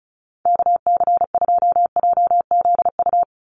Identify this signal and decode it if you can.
Morse code, heard as a keyed tone, about 34 wpm, 700 Hz. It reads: KC2JZU
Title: USA Call Signs 35wpm